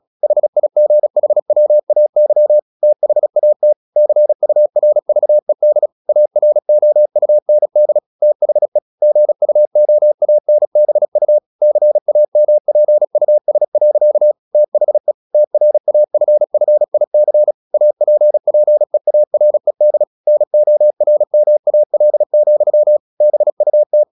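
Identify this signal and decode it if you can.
Morse code, heard as a keyed tone, about 36 wpm, 600 Hz. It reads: HIGHWAY THAT CURVED AROUND THE GUOANBU CAMPUS. THE TRAFFIC APPEARED NORMAL, BUT
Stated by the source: Morse practice at 35 wpm from CW med Gnister, sending Ken Follett's novel Never in English.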